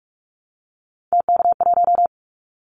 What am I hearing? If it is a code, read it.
Morse code, heard as a keyed tone, about 46 wpm, 700 Hz. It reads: TK1
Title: Callsign Prefixes for Top 100 DXCC - 45wpm